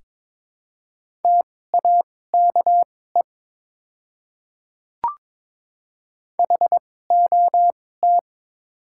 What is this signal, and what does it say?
Morse code, heard as a keyed tone, about 22 wpm, spaced as for 16 wpm, 700 Hz. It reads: TAKE HOT
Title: 2x Character Spacing - Top 200 Words - 22wpm